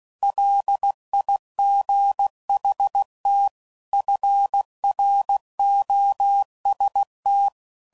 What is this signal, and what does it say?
Morse code, heard as a keyed tone, about 16 wpm, 800 Hz. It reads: LIGHT FROST
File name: CW Code Groups @15WPM / Weather Sets / Weather 4 24